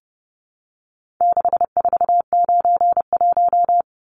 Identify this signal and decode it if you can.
Morse code, heard as a keyed tone, about 30 wpm, 700 Hz. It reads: 6491
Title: Numbers - 4-Digit 30wpm